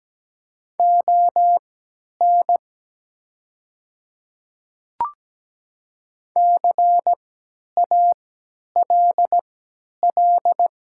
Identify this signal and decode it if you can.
Morse code, heard as a keyed tone, about 17 wpm, spaced as for 10 wpm, 700 Hz. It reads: ON CALL
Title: 3x Character Spacing - Top 100 Words - 17wpm